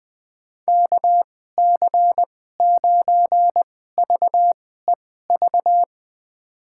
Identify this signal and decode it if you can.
Morse code, heard as a keyed tone, about 20 wpm, spaced as for 14 wpm, 700 Hz. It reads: KC9VEV
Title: USA Call Signs - 6 Characters - 2x Character Spacing - 20wpm